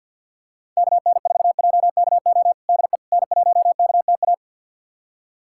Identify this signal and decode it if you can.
Morse code, heard as a keyed tone, about 50 wpm, 700 Hz. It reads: KN4JKO DE N1KTA
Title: USA Callsign DE Callsign 50wpm